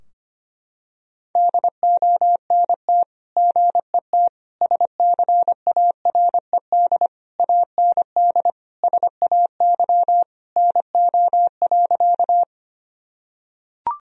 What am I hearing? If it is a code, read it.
Morse code, heard as a keyed tone, about 25 wpm, 700 Hz. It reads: DONT GET SCARED AND SAY NO.
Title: Sentences from Top 1000 Words - Review Part 4 - 25wpm